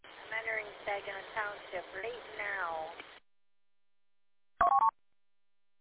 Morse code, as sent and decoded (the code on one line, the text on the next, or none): none